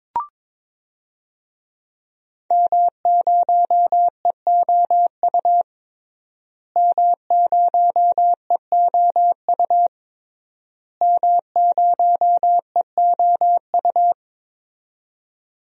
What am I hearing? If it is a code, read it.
Morse code, heard as a keyed tone, about 22 wpm, 700 Hz. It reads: M0EOU M0EOU M0EOU
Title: DXCC-100 Call Signs - 5 Characters - Repeated 3x - 22wpm